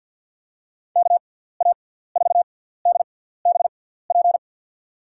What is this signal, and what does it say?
Morse code, heard as a keyed tone, about 53 wpm, 700 Hz. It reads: K A 4 D B P
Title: USA Call Signs - 6 Characters - 6x Character Spacing - 50wpm